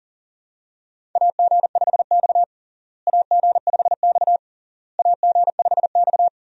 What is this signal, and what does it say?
Morse code, heard as a keyed tone, about 37 wpm, 700 Hz. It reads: AG5X AG5X AG5X
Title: USA Call Signs - 4 Characters - Repeated 3x - 40wpm